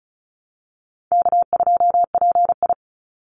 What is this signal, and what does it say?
Morse code, heard as a keyed tone, about 34 wpm, 700 Hz. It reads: K2PI
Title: Easy Call Signs Repeated 3x - 35wpm